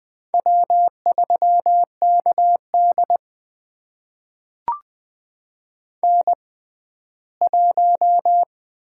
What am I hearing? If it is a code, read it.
Morse code, heard as a keyed tone, about 20 wpm, spaced as for 14 wpm, 700 Hz. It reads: W3KD N1
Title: USA Call Signs - 4 Characters - 6x Character Spacing - 20wpm